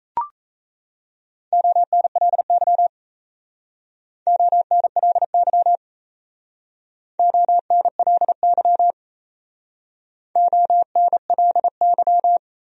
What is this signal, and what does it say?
Morse code, heard as a keyed tone, about 34 wpm, 700 Hz. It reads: ONLY ONLY ONLY ONLY
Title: Speed-Racing Top 300 Words 28wpm